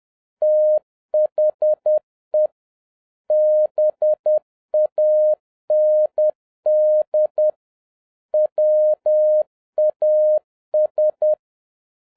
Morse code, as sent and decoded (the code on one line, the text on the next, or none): - .... / -... .- -. -.. / .-- .- ...
THE BAND WAS